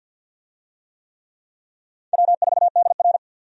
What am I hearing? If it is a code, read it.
Morse code, heard as a keyed tone, about 48 wpm, 700 Hz. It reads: W4DR
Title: USA Call Signs - 4 Characters - Repeated 2x - 50wpm